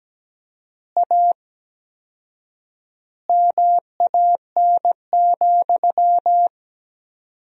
Morse code, heard as a keyed tone, about 17 wpm, 700 Hz. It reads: A MAN,
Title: 4x Word-Spacing - Sentences from Top 200 Words - 17wpm